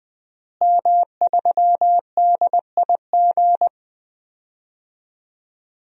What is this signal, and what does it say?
Morse code, heard as a keyed tone, about 20 wpm, 700 Hz. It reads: M3DIG
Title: DXCC-100 Call Signs - 5 Characters - Repeated 3x - 20wpm